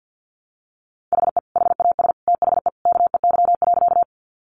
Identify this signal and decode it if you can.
Morse code, heard as a keyed tone, about 50 wpm, 700 Hz. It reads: HE HAS THE KEY.
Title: Sentences from Top 600 Words 50wpm